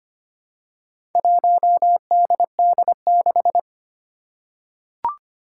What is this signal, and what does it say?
Morse code, heard as a keyed tone, about 25 wpm, 700 Hz. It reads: 1DD6
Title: Four Challenging Letter-Number Combinations 25wpm